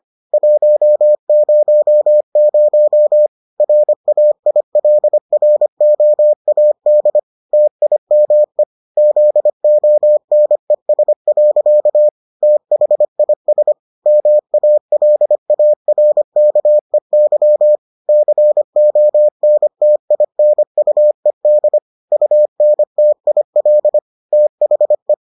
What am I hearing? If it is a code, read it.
Morse code, heard as a keyed tone, about 25 wpm, 600 Hz. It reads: 100 RAILROAD TIME ZONES. THIS MALARKEY CONTINUED UNTIL THE